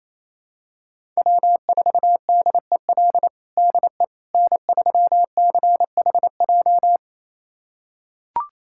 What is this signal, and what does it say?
Morse code, heard as a keyed tone, about 28 wpm, 700 Hz. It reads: W4DEL DE N3CHJ
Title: USA Callsign DE Callsign - 5 Characters - 28wpm